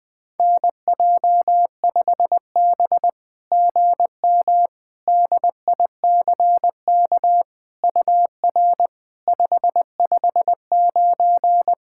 Morse code, heard as a keyed tone, about 20 wpm, 700 Hz. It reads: NJ5B GM DICK UR 559
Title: SOTA QSO Practice 20wpm